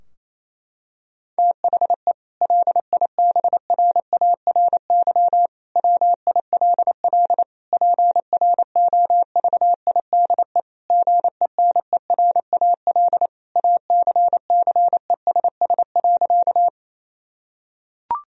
Morse code, heard as a keyed tone, about 28 wpm, 700 Hz. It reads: THE LIBRARY WILL PROVIDE GENERAL ACCESS.
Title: Sentences from Top 1000 Words - Review Part 4 - 28wpm